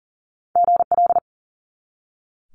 Morse code, heard as a keyed tone, about 41 wpm, 700 Hz. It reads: GL
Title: Multi-letter Phonemes 40wpm